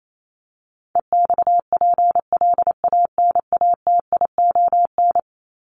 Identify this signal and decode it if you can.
Morse code, heard as a keyed tone, about 28 wpm, 700 Hz. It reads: EXPLANATION
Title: Top 1751-2000 Words 28wpm